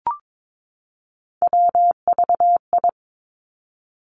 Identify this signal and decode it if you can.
Morse code, heard as a keyed tone, about 22 wpm, 700 Hz. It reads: WVI
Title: Three Challenging Letter-Number Combinations 22wpm